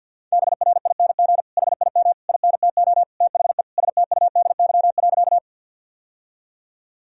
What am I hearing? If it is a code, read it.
Morse code, heard as a keyed tone, about 50 wpm, 700 Hz. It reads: BRING HIM INTO THE STUDY.
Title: Sentences from Top 1000 Words - Review Part 1 - 50wpm